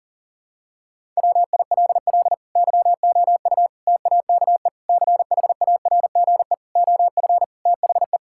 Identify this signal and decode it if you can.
Morse code, heard as a keyed tone, about 40 wpm, 700 Hz. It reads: WILL YOU TAKE CHARGE OF THE
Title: Sentences from Top 1000 Words - Review Part 6 - 40wpm